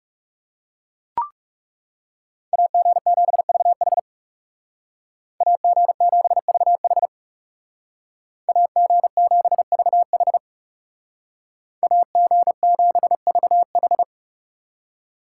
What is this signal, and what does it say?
Morse code, heard as a keyed tone, about 35 wpm, 700 Hz. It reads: AG7VH AG7VH AG7VH AG7VH
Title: Speed Racing - USA 2x2 Call Signs 30wpm